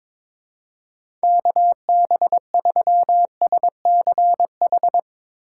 Morse code, heard as a keyed tone, about 22 wpm, 700 Hz. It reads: KB3SCH
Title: Speed Racing - USA 2x3 Call Signs 22wpm